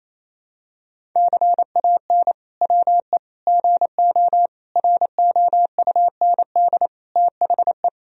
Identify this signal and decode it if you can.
Morse code, heard as a keyed tone, about 28 wpm, 700 Hz. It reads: CAN WE GO ROUND THE
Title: Sentences from Top 1000 Words - Review Part 1 - 28wpm